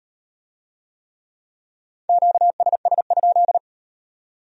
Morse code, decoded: QSS?

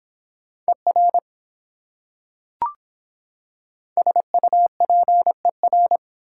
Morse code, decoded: ER SUPER

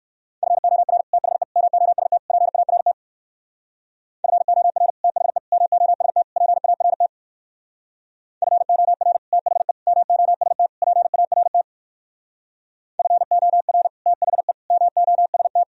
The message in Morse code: FOR THE MOST PART FOR THE MOST PART FOR THE MOST PART FOR THE MOST

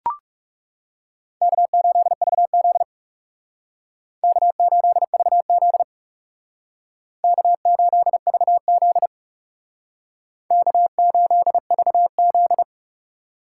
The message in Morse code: K8VZ K8VZ K8VZ K8VZ

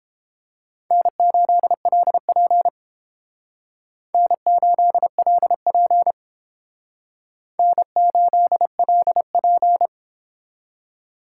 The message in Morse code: N8LP N8LP N8LP

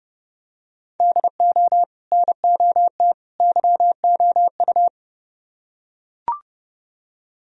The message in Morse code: DO NOT YOU

